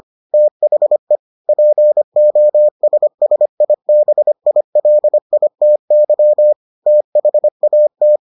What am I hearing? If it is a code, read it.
Morse code, heard as a keyed tone, about 25 wpm, 600 Hz. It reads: THE POSSIBILITY THAT